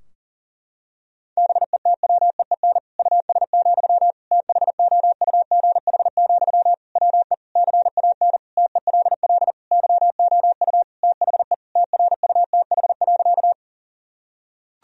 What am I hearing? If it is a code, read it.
Morse code, heard as a keyed tone, about 40 wpm, 700 Hz. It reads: BETWEEN US, THOUGH, WE CAN TELL YOU THE TRUTH.
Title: Sentences from Top 1000 Words - Review Part 2 - 40wpm